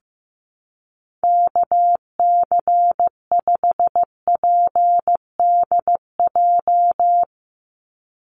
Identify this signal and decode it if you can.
Morse code, heard as a keyed tone, about 15 wpm, 700 Hz. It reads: KC5PDJ
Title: USA Call Signs 15wpm